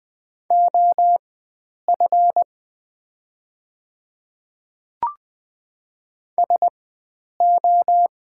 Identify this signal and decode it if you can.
Morse code, heard as a keyed tone, about 20 wpm, spaced as for 9 wpm, 700 Hz. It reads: OF SO
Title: 4x Character Spacing - Top 300 Words - 20wpm